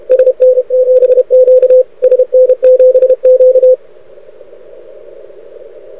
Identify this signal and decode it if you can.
Morse code, heard as a keyed tone, about 31 wpm, 500 Hz, with radio noise. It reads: SN7Q SN7Q
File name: rec contest / arrl cw 2018 / test 32